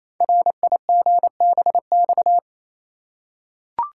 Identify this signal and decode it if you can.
Morse code, heard as a keyed tone, about 27 wpm, 700 Hz. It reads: RIGBX
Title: Code Groups - 5 Letters - 28wpm